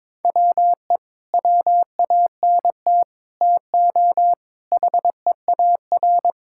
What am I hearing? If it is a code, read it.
Morse code, heard as a keyed tone, about 22 wpm, 700 Hz. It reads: WE WANT TO HEAR
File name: practice / 22wpm / Sets of 8 Words - 22wpm